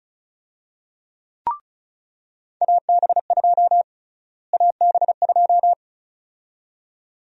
Morse code, heard as a keyed tone, about 36 wpm, 700 Hz. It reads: AB2 AB2
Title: USA Call Sign Prefixes - Repeated 2x - 35wpm